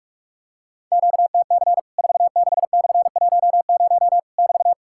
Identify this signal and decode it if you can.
Morse code, heard as a keyed tone, about 48 wpm, 700 Hz. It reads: QTC 46/10 <BT>
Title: Worked All Europe DX Contest - QTC Exchange - 45wpm